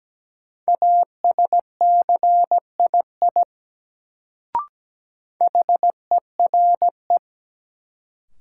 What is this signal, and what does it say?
Morse code, heard as a keyed tone, about 17 wpm, 700 Hz. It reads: ASCII HERE
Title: Sets of 5 Words - New Word Drill - 17wpm